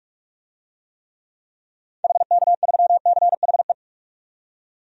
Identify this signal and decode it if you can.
Morse code, heard as a keyed tone, about 46 wpm, 700 Hz. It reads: HK3CHE
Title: DXCC-100 Call Signs - 6 Characters - 45wpm